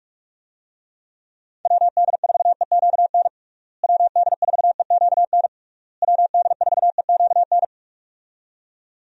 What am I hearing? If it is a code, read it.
Morse code, heard as a keyed tone, about 46 wpm, 700 Hz. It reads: WD4EQN WD4EQN WD4EQN
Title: USA Call Signs - 6 Characters - Repeated 3x - 45wpm